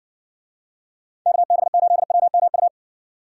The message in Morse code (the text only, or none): KB8WMU